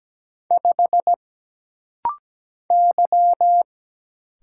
Morse code, heard as a keyed tone, about 17 wpm, 700 Hz. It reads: 5 Y
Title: Single Letter-Number Alt - Rapid-Fire 17wpm